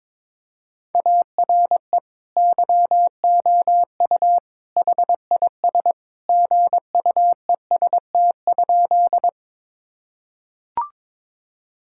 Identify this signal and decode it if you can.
Morse code, heard as a keyed tone, about 22 wpm, 700 Hz. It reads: ARE YOU HIS GUEST?
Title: Sentences from Top 1600 Words 22wpm